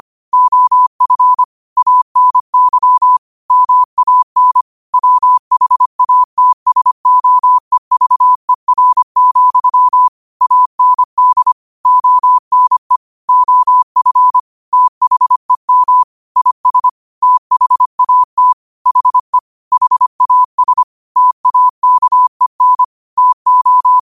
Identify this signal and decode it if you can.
Morse code, heard as a keyed tone, about 25 wpm, 1000 Hz. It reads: OF ANY MAN WHATSOEVER, AND ONE OF THEM IS THAT HE HAS TAKEN TO
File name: 25wpm